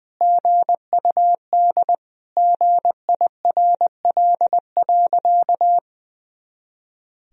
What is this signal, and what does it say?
Morse code, heard as a keyed tone, about 20 wpm, 700 Hz. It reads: GUD GIRL.
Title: Sentences from Top 400 Words and Abbreviations 20wpm